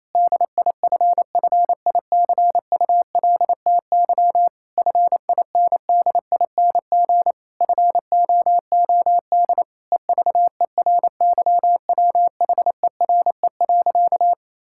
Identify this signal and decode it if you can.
Morse code, heard as a keyed tone, about 28 wpm, 700 Hz. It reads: DIFFICULTY FINDING FOOD EVERYWHERE.
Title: Sentences from Top 1600 Words 28wpm